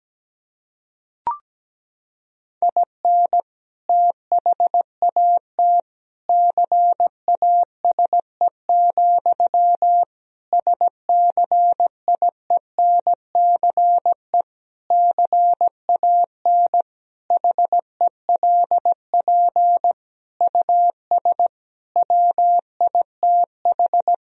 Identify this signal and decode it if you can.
Morse code, heard as a keyed tone, about 17 wpm, 700 Hz. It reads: IN THAT CASE, SCIENCE CAN HELP US WITH
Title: Sentences from Top 1000 Words - Review Part 2 - 17wpm